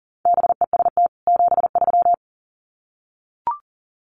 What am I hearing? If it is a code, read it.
Morse code, heard as a keyed tone, about 40 wpm, 700 Hz. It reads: BEST 73
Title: Common QSO Phrases 40wpm